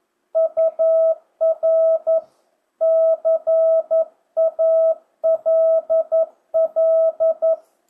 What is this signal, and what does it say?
Morse code, heard as a keyed tone, about 11 wpm, 660 Hz. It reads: UR CALL
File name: jh8cbh / 2025cw